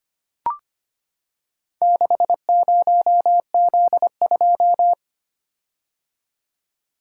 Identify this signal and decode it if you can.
Morse code, heard as a keyed tone, about 25 wpm, 700 Hz. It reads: 60Z2
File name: practice / 25wpm / Four Letter-Number Combinations - 25wpm